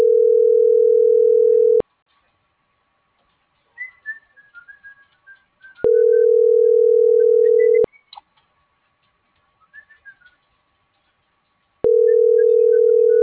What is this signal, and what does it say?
Ringback tone (North America), 440 + 480 Hz.